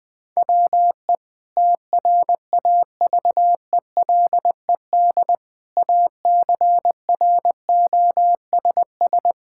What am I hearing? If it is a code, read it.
Morse code, heard as a keyed tone, about 21 wpm, 700 Hz. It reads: WE TRAVELED ACROSS
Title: Sentences with 2-Word Phrases - 20wpm